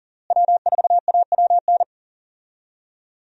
Morse code, heard as a keyed tone, about 40 wpm, 700 Hz. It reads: W4AWN
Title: USA Call Signs - 5 Characters - Repeated 3x - 40wpm